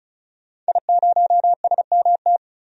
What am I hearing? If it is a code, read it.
Morse code, heard as a keyed tone, about 34 wpm, 700 Hz. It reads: I0SMT
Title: DXCC-100 Call Signs - 5 Characters - 35wpm